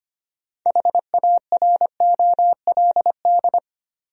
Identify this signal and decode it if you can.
Morse code, heard as a keyed tone, about 25 wpm, 700 Hz. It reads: HAROLD
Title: USA Common Names - 25wpm